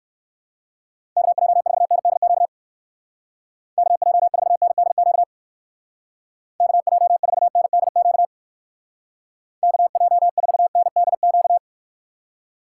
Morse code, heard as a keyed tone, about 55 wpm, 700 Hz. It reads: KJ4NDQ KJ4NDQ KJ4NDQ KJ4NDQ
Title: Speed Racing - USA Call Signs 45wpm